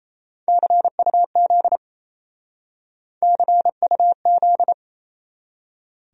Morse code, heard as a keyed tone, about 33 wpm, 700 Hz. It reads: CUZ CUZ
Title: Speed-Racing Abbreviations - 28wpm